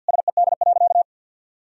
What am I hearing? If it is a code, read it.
Morse code, heard as a keyed tone, about 46 wpm, 700 Hz. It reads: SED.
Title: Sentences from Top 500 Words and Abbreviations 50wpm